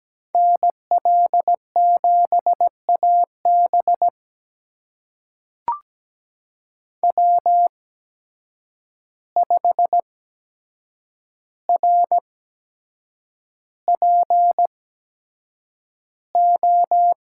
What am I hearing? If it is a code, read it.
Morse code, heard as a keyed tone, about 17 wpm, spaced as for 7 wpm, 700 Hz. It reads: NL7AB W5RPO
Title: USA Call Signs - 5 Characters - 8x Character Spacing - 17wpm